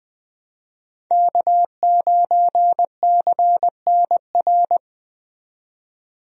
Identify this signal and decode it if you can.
Morse code, heard as a keyed tone, about 20 wpm, 700 Hz. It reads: K9CNR
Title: USA Call Signs - 5 Characters - 20wpm